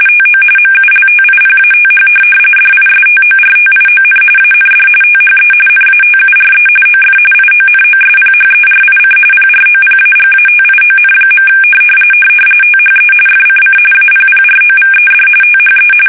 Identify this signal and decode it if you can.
RTTY, about 45.45 baud. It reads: TO MULTIPATH DISTORTION AND THE NATURE OF IONOSPHERIC PROPAGATION. THE FCC APPROVED THE
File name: fsk850 / teletype-RTTY